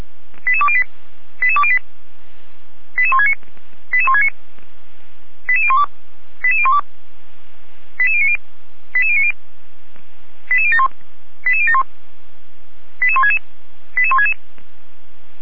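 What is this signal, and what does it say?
Five-tone selective call (ZVEI), heard as quick runs of five tones, tones of about 70 ms. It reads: 88208 88208 88169 88169 88012 88012 88090 88090 88071 88071 88166 88166
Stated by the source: ZVEI five-tone selective call received on SDR, from feuerwehr-orlamuende fire brigade.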